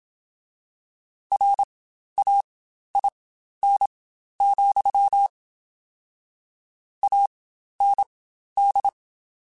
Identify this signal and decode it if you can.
Morse code, heard as a keyed tone, about 26 wpm, spaced as for 12 wpm, 800 Hz. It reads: RAIN, AND
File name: Spaced Characters / Spaced Characters-mp3s / Weather5 5 sp @25